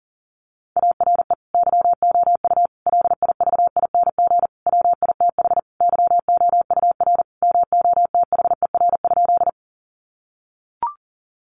Morse code, heard as a keyed tone, about 40 wpm, 700 Hz. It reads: ARE YOU LIVING WITH YOUR MOTHER?
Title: Sentences from Top 1100 Words 40wpm